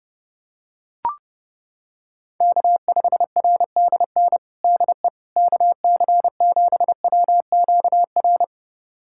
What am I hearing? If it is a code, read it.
Morse code, heard as a keyed tone, about 30 wpm, 700 Hz. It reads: K5RDN DE KC7WQR
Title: USA Callsign DE Callsign 30wpm